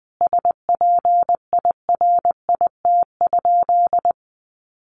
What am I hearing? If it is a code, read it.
Morse code, heard as a keyed tone, about 20 wpm, 700 Hz. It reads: SPIRIT?
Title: Sentences from Top 1400 Words 20wpm